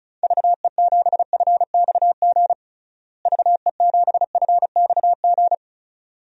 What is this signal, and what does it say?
Morse code, heard as a keyed tone, about 36 wpm, 700 Hz. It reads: VE7FXG VE7FXG
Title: DXCC-100 Call Signs - 6 Characters - Repeated 2x - 35wpm